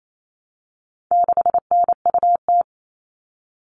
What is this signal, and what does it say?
Morse code, heard as a keyed tone, about 28 wpm, 700 Hz. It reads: 6NUT
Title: All Cut Numbers - 4-Digit 28wpm